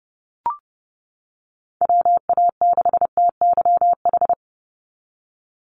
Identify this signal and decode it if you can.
Morse code, heard as a keyed tone, about 31 wpm, 700 Hz. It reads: WA6TYH
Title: USA Call Signs 30wpm